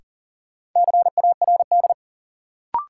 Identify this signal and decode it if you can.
Morse code, heard as a keyed tone, about 40 wpm, 700 Hz. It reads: CARD
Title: Sentences from Top 600 Words - New Word Drill - 40wpm